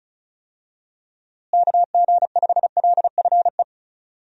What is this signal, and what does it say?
Morse code, heard as a keyed tone, about 36 wpm, 700 Hz. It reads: KG5LFE
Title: USA Call Signs - 6 Characters - Repeated 3x - 35wpm